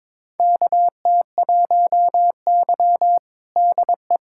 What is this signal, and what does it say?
Morse code, heard as a keyed tone, about 22 wpm, 700 Hz. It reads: KT1Y DE
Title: USA Callsign DE Callsign - 4 Characters - 22wpm